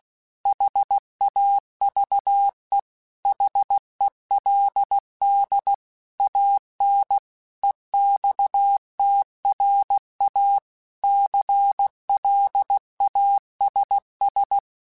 Morse code, heard as a keyed tone, about 16 wpm, 800 Hz. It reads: HAVE HELD AN EXTRA CLASS
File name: QSOs / QSOs-mp3s / kc7haa@15